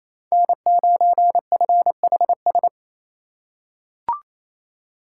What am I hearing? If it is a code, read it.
Morse code, heard as a keyed tone, about 28 wpm, 700 Hz. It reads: N9FHS
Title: USA Call Signs - 5 Characters - 2x Character Spacing - 28wpm